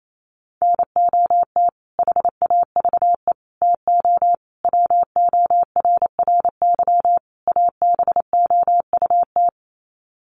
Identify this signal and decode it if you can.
Morse code, heard as a keyed tone, about 28 wpm, 700 Hz. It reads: NOT HAVE TO WORRY ABOUT
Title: Sets of 5 Words 28wpm